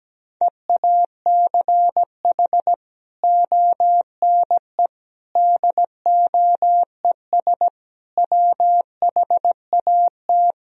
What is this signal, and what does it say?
Morse code, heard as a keyed tone, about 17 wpm, 700 Hz. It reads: EACH ONE DOES WHAT
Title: Sentences from Top 1000 Words - Review Part 1 - 17wpm